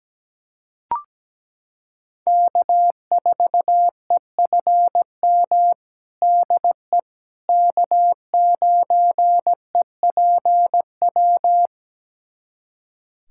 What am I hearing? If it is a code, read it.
Morse code, heard as a keyed tone, about 17 wpm, 700 Hz. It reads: K4EFM DE K9EPW
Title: USA Callsign DE Callsign - 5 Characters - 17wpm